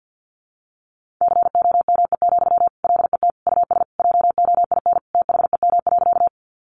Morse code, heard as a keyed tone, about 50 wpm, 700 Hz. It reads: COME, LET US JOIN THEM.
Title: Sentences from Top 1400 Words 50wpm